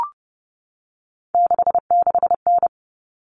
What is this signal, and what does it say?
Morse code, heard as a keyed tone, about 30 wpm, 700 Hz. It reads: 66N
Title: All Cut Numbers - 3-Digit 30wpm